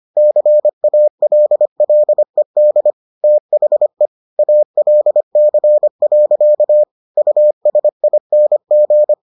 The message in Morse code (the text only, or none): CALLED THE ALC. USING